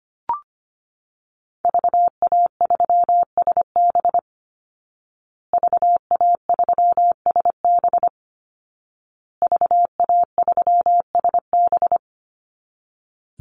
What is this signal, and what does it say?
Morse code, heard as a keyed tone, about 25 wpm, 700 Hz. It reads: VA3SB VA3SB VA3SB
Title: Call Signs Repeated 2x - 25wpm